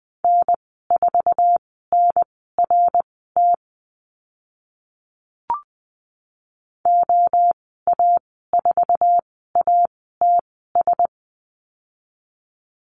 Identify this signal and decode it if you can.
Morse code, heard as a keyed tone, about 20 wpm, spaced as for 14 wpm, 700 Hz. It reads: N4NRT OA4ATS
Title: 055 Lesson 10g - Letter r - Call Signs 20wpm